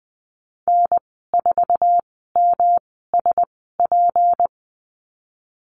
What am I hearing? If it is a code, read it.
Morse code, heard as a keyed tone, about 20 wpm, spaced as for 15 wpm, 700 Hz. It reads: N4MSP
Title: 179 Lesson 29g - Slash - Call Signs 20wpm